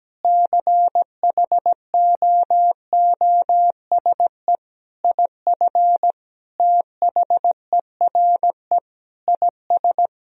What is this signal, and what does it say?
Morse code, heard as a keyed tone, about 17 wpm, 700 Hz. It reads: CHOOSE IF THERE IS
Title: Sentences from Top 1000 Words - Review Part 4 - 17wpm